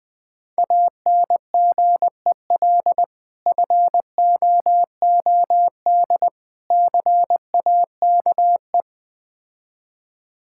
Morse code, heard as a keyed tone, about 20 wpm, 700 Hz. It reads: ANGEL FOOD CAKE
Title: Difficult Words with Hints - 20wpm